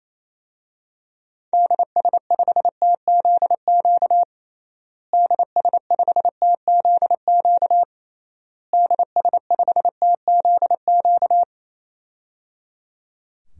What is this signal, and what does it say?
Morse code, heard as a keyed tone, about 28 wpm, 700 Hz. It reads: DS5TZQ DS5TZQ DS5TZQ
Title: DXCC-100 Call Signs - 6 Characters - Repeated 3x - 28wpm